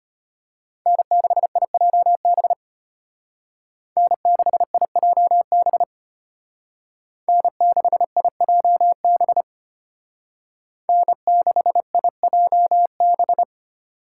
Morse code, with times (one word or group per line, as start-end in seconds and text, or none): 0.9-2.5 N6IJB
4.0-5.8 N6IJB
7.3-9.4 N6IJB
10.9-13.4 N6IJB